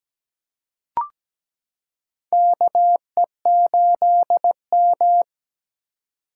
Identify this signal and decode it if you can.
Morse code, heard as a keyed tone, about 17 wpm, 700 Hz. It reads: KE8M
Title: K1USN SST QSO Practice - 17wpm